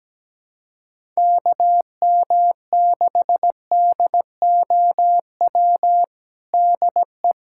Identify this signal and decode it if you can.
Morse code, heard as a keyed tone, about 17 wpm, 700 Hz. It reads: KM6DOW DE
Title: USA Callsign DE Callsign - 6 Characters - 17wpm